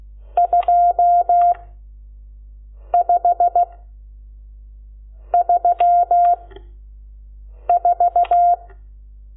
Morse code, heard as a keyed tone, about 16 wpm, 700 Hz. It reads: 2 5 3 4
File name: Lecciones / CLASE Nº12